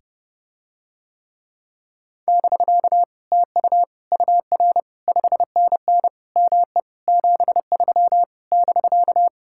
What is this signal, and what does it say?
Morse code, heard as a keyed tone, about 30 wpm, 700 Hz. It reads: <BK> TU UR 5NN ME 73 <BK>